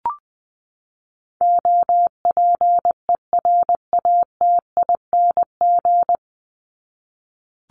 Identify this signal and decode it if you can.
Morse code, heard as a keyed tone, about 20 wpm, 700 Hz. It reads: OPERATING
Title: ING Suffix - Top 100 Words with Base Words 20wpm